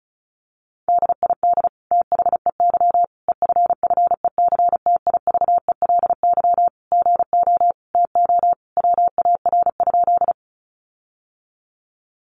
Morse code, heard as a keyed tone, about 34 wpm, 700 Hz. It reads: DID THEY EFFECTIVELY GO TO WAR?
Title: Sentences from Top 1400 Words 35wpm